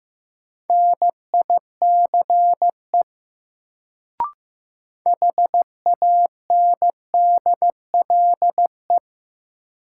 Morse code, heard as a keyed tone, about 15 wpm, 700 Hz. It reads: NICE HANDLE
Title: Sentences from Top 1100 Words - New Word Drill - 15wpm